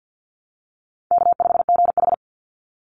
Morse code, heard as a keyed tone, about 50 wpm, 700 Hz. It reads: K5GH
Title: Easy Call Signs - 50wpm